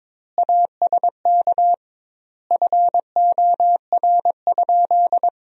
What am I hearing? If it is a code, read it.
Morse code, heard as a keyed tone, about 22 wpm, 700 Hz. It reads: ASK FOR?
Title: 2x Word-Spacing - Sentences from Top 200 Words - 22wpm